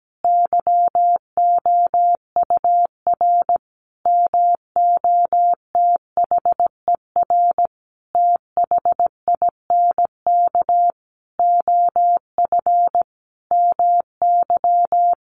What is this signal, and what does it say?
Morse code, heard as a keyed tone, about 17 wpm, 700 Hz. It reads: YOUR MOTHER THINK OF MY
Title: Sentences from Top 1400 Words 17wpm